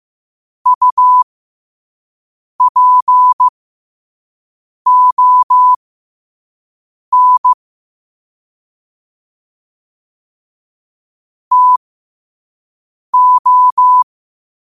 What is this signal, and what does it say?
Morse code, heard as a keyed tone, about 15 wpm, spaced as for 5 wpm, 1000 Hz. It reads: UPON TO
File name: qotdfiles / 5wpm